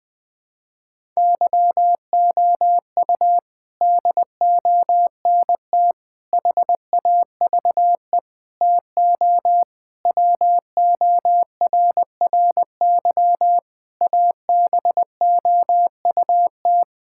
Morse code, heard as a keyed tone, about 21 wpm, 700 Hz. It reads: YOU DONT HAVE TO WORRY ABOUT
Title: Sets of 6 Words - 20wpm